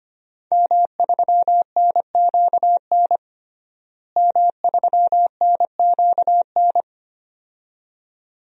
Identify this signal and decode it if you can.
Morse code, heard as a keyed tone, about 25 wpm, 700 Hz. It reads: M3NQN M3NQN
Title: DXCC-100 Call Signs - 5 Characters - Repeated 2x - 25wpm